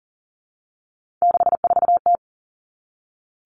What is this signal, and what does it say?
Morse code, heard as a keyed tone, about 40 wpm, 700 Hz. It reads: 64T
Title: Common Cut Numbers - 3-Digit 40wpm